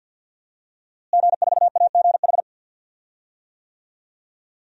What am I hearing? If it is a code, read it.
Morse code, heard as a keyed tone, about 53 wpm, 700 Hz. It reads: G4AGH
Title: DXCC-100 Call Signs - 5 Characters - Repeated 3x - 50wpm